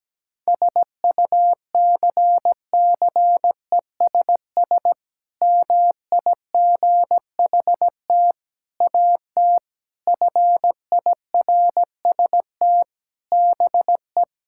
Morse code, heard as a keyed tone, about 17 wpm, 700 Hz. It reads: SUCCESS MIGHT AT FIRST BE
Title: Sentences from Top 1000 Words - Review Part 6 - 17wpm